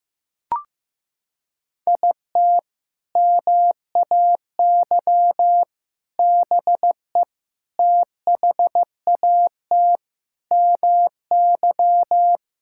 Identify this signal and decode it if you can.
Morse code, heard as a keyed tone, about 15 wpm, 700 Hz. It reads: IT MAY BE THAT MY
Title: Sentences from Top 1600 Words 15wpm